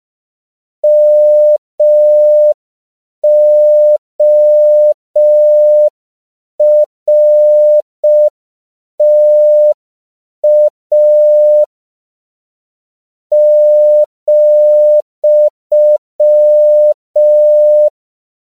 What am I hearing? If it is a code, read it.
Morse code, heard as a keyed tone, about 5 wpm, 600 Hz. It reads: MORTA ,
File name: 5PPM